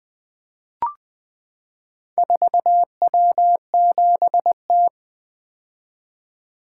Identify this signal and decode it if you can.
Morse code, heard as a keyed tone, about 20 wpm, 700 Hz. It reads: 4W7T
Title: Four Letter-Number Combinations - 20wpm